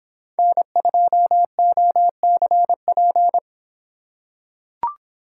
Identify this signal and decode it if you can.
Morse code, heard as a keyed tone, about 23 wpm, 700 Hz. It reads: N2OCP